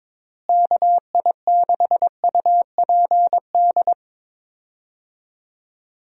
Morse code, heard as a keyed tone, about 22 wpm, 700 Hz. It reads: KI6UPD
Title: USA Call Signs - 6 Characters - Repeated 3x - 22wpm